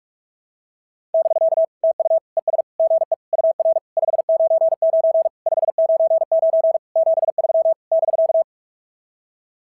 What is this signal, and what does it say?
Morse code, heard as a keyed tone, about 44 wpm, 660 Hz. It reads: <BK> TU ES GE UR 599 599 73 <BK>
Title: SOTA QSO Practice 45wpm